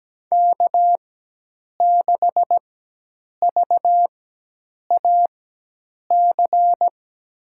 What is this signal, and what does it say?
Morse code, heard as a keyed tone, about 17 wpm, spaced as for 8 wpm, 700 Hz. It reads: K6VAC
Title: USA Call Signs - 5 Characters - 4x Character Spacing - 17wpm